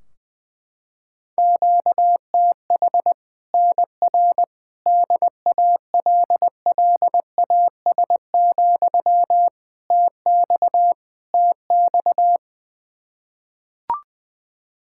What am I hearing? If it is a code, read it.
Morse code, heard as a keyed tone, about 20 wpm, 700 Hz. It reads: QTH NR DALLAS, TX TX
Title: Practice QSO - Exchange 1 - Beginning - 20wpm